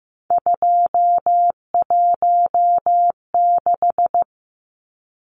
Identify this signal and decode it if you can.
Morse code, heard as a keyed tone, about 15 wpm, 700 Hz. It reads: 216
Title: Numbers - 3-Digit 15wpm